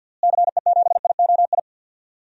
Morse code, heard as a keyed tone, about 53 wpm, 700 Hz. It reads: CE7IOI
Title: DXCC-100 Call Signs - 6 Characters - Repeated 3x - 50wpm